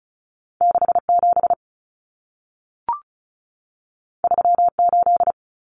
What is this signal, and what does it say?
Morse code, heard as a keyed tone, about 36 wpm, 700 Hz. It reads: 67 38
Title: Common Cut Numbers - 2-Digit 35wpm